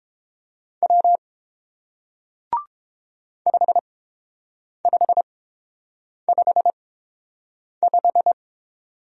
Morse code, heard as a keyed tone, about 30 wpm, spaced as for 7 wpm, 700 Hz. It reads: W 5555